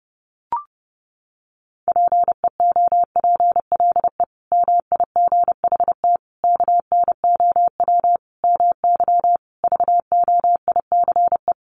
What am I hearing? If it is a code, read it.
Morse code, heard as a keyed tone, about 30 wpm, 700 Hz. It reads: PEOPLE MIGHT KNOW MY VOICE
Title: Sentences from Top 500 Words 30wpm